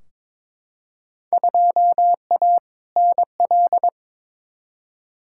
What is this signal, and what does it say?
Morse code, heard as a keyed tone, about 22 wpm, 700 Hz. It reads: 2A NL
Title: Field Day Exchanges Encore - 22wpm